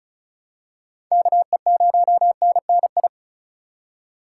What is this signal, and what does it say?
Morse code, heard as a keyed tone, about 34 wpm, 700 Hz. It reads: KE0NNI
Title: USA Call Signs - 6 Characters - 4x Character Spacing - 35wpm